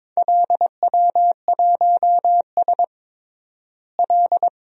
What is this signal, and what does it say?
Morse code, heard as a keyed tone, about 22 wpm, 700 Hz. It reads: LW1S L
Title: DXCC-100 Call Signs - 4 Characters - Repeated 2x - 22wpm